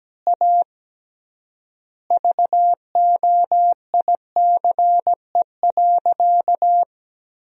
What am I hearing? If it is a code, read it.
Morse code, heard as a keyed tone, about 17 wpm, 700 Hz. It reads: A VOICE.
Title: 3x Word-Spacing - Sentences from Top 500 Words - 17wpm